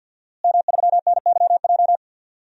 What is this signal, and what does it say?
Morse code, heard as a keyed tone, about 47 wpm, 700 Hz. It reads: M3NYJ
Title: DXCC-100 Call Signs - 5 Characters - Repeated 2x - 50wpm